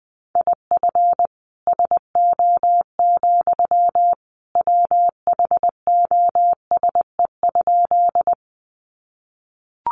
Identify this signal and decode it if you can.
Morse code, heard as a keyed tone, about 19 wpm, 700 Hz. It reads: IF SO, WHOSE?
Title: Sentences from Top 1200 Words 20wpm